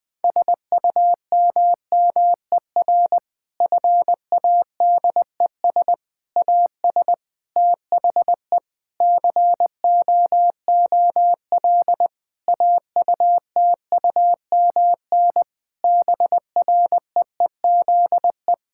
Morse code, heard as a keyed tone, about 20 wpm, 700 Hz. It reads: SUMMER FADES AS THE COOL AUTUMN BREEZE
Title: Difficult Sentences with Hints - 20wpm